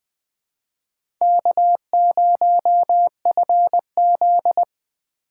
Morse code, heard as a keyed tone, about 20 wpm, 700 Hz. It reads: K0FZ
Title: USA Call Signs - 4 Characters - 2x Character Spacing - 20wpm